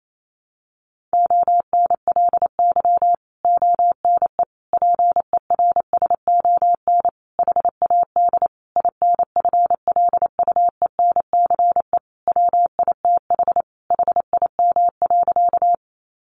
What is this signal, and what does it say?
Morse code, heard as a keyed tone, about 28 wpm, 700 Hz. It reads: ONLY ONE PERSON HAD INFLUENCE WITH HIM.
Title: Sentences from Top 1100 Words 28wpm